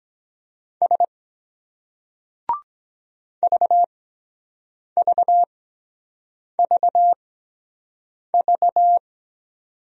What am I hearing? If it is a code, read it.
Morse code, heard as a keyed tone, about 23 wpm, spaced as for 7 wpm, 700 Hz. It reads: S VVVV